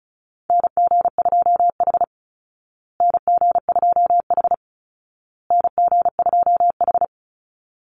Morse code, heard as a keyed tone, about 34 wpm, 700 Hz. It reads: NG2H NG2H NG2H
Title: Call Signs Repeated 2x - 35wpm